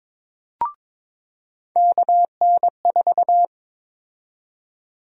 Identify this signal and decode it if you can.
Morse code, heard as a keyed tone, about 22 wpm, 700 Hz. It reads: KN4
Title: USA Call Sign Prefixes - 22wpm